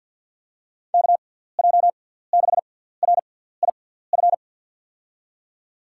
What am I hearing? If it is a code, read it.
Morse code, heard as a keyed tone, about 48 wpm, spaced as for 17 wpm, 700 Hz. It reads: KJ6RIF
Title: USA Call Signs - 6 Characters - 6x Character Spacing - 50wpm